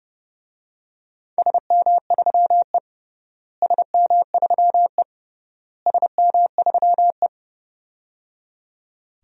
Morse code, heard as a keyed tone, about 30 wpm, 700 Hz. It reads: SM3E SM3E SM3E
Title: DXCC-100 Call Signs - 4 Characters - Repeated 3x - 30wpm